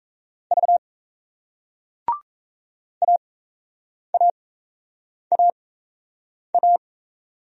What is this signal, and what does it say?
Morse code, heard as a keyed tone, about 38 wpm, spaced as for 8 wpm, 700 Hz. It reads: V AAAA